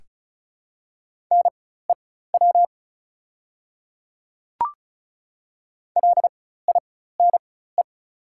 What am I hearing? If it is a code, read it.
Morse code, heard as a keyed tone, about 36 wpm, spaced as for 16 wpm, 690 Hz. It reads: NEW LINE
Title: 4x Character Spacing - Top 200 Words - 35wpm